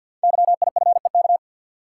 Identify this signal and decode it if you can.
Morse code, heard as a keyed tone, about 52 wpm, 700 Hz. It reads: YIFEK
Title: Code Groups - 5 Letters - 50wpm